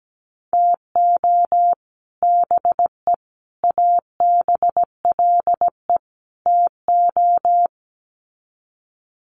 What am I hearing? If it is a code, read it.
Morse code, heard as a keyed tone, about 17 wpm, 700 Hz. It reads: TO BE ABLE TO
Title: Sets of 4 Words 17wpm